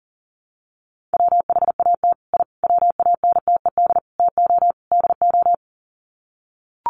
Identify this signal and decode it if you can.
Morse code, heard as a keyed tone, about 40 wpm, 700 Hz. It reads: WHAT I WANTED TO DO
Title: Sets of 5 Words Encore2 40wpm